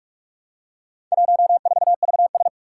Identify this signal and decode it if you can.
Morse code, heard as a keyed tone, about 44 wpm, 700 Hz. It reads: 14VS
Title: Four Challenging Letter-Number Combinations 45wpm